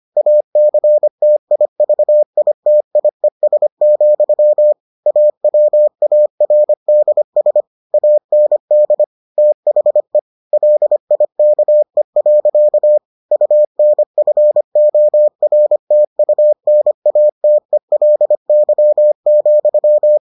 Morse code, heard as a keyed tone, about 25 wpm, 600 Hz. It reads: ACTIVITIES, AWARDS AND THE LIKE. UNFORTUNATELY,